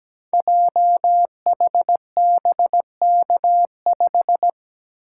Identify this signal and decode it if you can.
Morse code, heard as a keyed tone, about 17 wpm, 700 Hz. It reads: JHBK5
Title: Code Groups - 5 Letter:Numbers - 17wpm